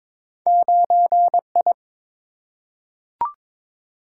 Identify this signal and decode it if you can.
Morse code, heard as a keyed tone, about 23 wpm, 700 Hz. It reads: 9I
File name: practice / 22wpm / Two Letter-Number Combinations - 22wpm